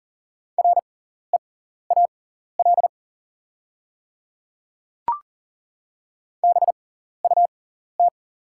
Morse code, heard as a keyed tone, about 38 wpm, spaced as for 14 wpm, 700 Hz. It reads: REAL BUT
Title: 6x Character Spacing - Top 300 Words - 40wpm